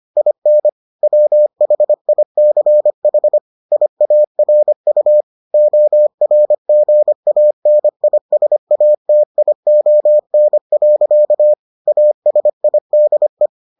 Morse code, heard as a keyed tone, about 25 wpm, 600 Hz. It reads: IN WHICH IARU ORGANISATION. ASIDE